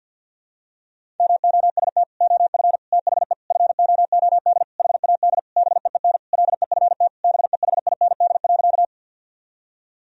Morse code, heard as a keyed tone, about 53 wpm, 700 Hz. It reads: MOST OF THE FOOD HAD BEEN LEFT BEHIND.
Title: Sentences from Top 1000 Words - Review Part 1 - 50wpm